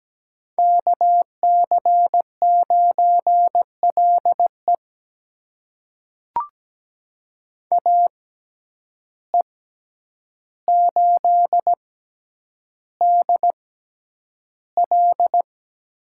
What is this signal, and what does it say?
Morse code, heard as a keyed tone, about 17 wpm, spaced as for 9 wpm, 700 Hz. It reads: KC9LE AE8DL
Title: USA Call Signs - 5 Characters - 6x Character Spacing - 17wpm